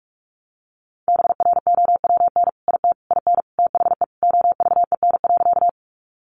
Morse code, heard as a keyed tone, about 44 wpm, 700 Hz. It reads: BROWN IT IN THE OVEN.
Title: Sentences from Top 1200 Words 45wpm